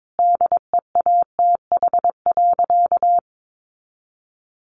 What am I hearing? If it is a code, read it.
Morse code, heard as a keyed tone, about 22 wpm, 700 Hz. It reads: DEATH.
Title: Sentences from Top 1500 Words 22wpm